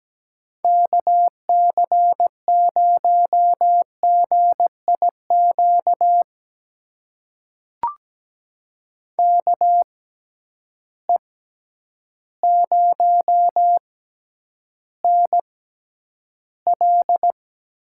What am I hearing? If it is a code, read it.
Morse code, heard as a keyed tone, about 17 wpm, spaced as for 9 wpm, 700 Hz. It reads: KC0GIQ KE0NL
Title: USA Call Signs - 6 Characters - 6x Character Spacing - 17wpm